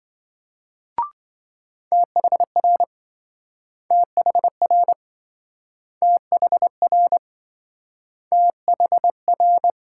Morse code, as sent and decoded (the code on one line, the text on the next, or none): - .... .-. / - .... .-. / - .... .-. / - .... .-.
THR THR THR THR